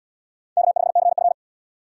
Morse code, heard as a keyed tone, about 69 wpm, 680 Hz. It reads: K5CF